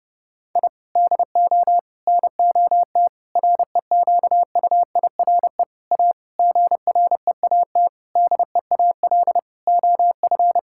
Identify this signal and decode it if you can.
Morse code, heard as a keyed tone, about 30 wpm, 700 Hz. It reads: I DO NOT REQUIRE A GREAT DEAL OF
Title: Sentences from Top 1000 Words - Review Part 6 - 30wpm